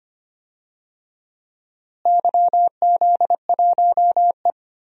Morse code, heard as a keyed tone, about 25 wpm, 700 Hz. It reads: YZ1E
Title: DXCC-100 Call Signs - 4 Characters - Repeated 2x - 25wpm